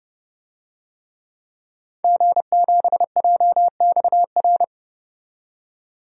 Morse code, heard as a keyed tone, about 29 wpm, 700 Hz. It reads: G7JXR
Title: DXCC-100 Call Signs - 5 Characters - 30wpm